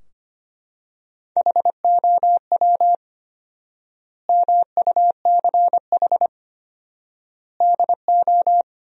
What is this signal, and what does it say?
Morse code, heard as a keyed tone, about 25 wpm, 700 Hz. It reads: HOW MUCH DO
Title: 4x Word-Spacing - Sentences from Top 200 Words - 25wpm